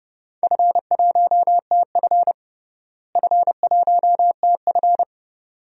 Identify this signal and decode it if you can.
Morse code, heard as a keyed tone, about 30 wpm, 700 Hz. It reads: F1TF F1TF
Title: DXCC-100 Call Signs - 4 Characters - Repeated 2x - 30wpm